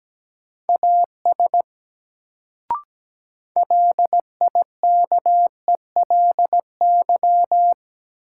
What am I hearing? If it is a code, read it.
Morse code, heard as a keyed tone, about 17 wpm, 700 Hz. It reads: AS LIKELY
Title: Sets of 2 Words - New Word Drill - 17wpm